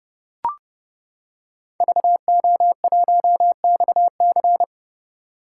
Morse code, heard as a keyed tone, about 30 wpm, 700 Hz. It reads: VO1XC
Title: K1USN SST QSO Practice - 30wpm